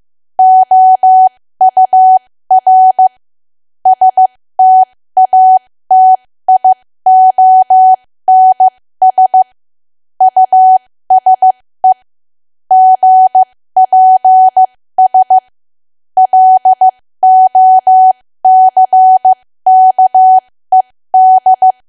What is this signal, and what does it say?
Morse code, heard as a keyed tone, about 15 wpm, spaced as for 13 wpm, 750 Hz. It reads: OUR STATIONS USE GPS LOCKED
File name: cwa / W1AW-Level2 / 13WPM